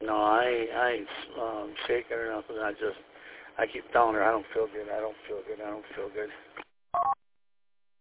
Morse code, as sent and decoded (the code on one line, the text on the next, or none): none